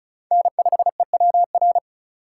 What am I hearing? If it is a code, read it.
Morse code, heard as a keyed tone, about 44 wpm, 700 Hz. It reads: N5EWR
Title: Speed Racing - USA 1x3 Call Signs 35wpm